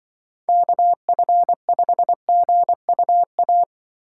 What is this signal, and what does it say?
Morse code, heard as a keyed tone, about 25 wpm, 700 Hz. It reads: KF5GUA
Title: Speed Racing - USA Call Signs 20wpm